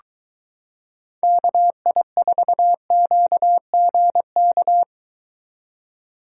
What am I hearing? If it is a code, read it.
Morse code, heard as a keyed tone, about 22 wpm, 700 Hz. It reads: KI4QGK